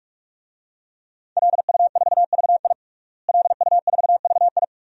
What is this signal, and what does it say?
Morse code, heard as a keyed tone, about 46 wpm, 700 Hz. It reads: LU4VI LU4VI
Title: DXCC-100 Call Signs - 5 Characters - Repeated 2x - 45wpm